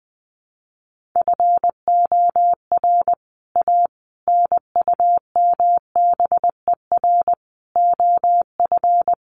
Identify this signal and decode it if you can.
Morse code, heard as a keyed tone, about 19 wpm, 700 Hz. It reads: FOR A NUMBER OF
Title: Sets of 5 Words Encore2 20wpm